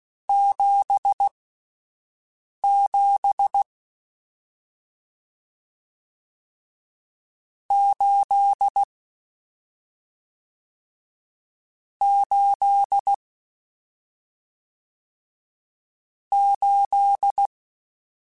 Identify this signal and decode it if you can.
Morse code, heard as a keyed tone, about 16 wpm, spaced as for 5 wpm, 800 Hz. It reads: 77 8 8 8
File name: Misc / Misc-mp3s / Numb @5